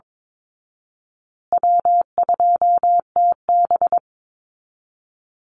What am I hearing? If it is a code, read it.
Morse code, heard as a keyed tone, about 22 wpm, 700 Hz. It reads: W2TB
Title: Call Signs Repeated 2x - 22wpm